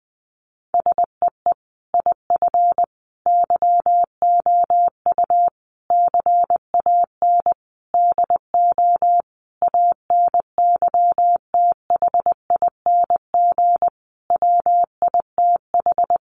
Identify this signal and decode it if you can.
Morse code, heard as a keyed tone, about 20 wpm, 700 Hz. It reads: SEE IF YOU CAN DO ANYTHING WITH
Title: Sentences from Top 1000 Words 20wpm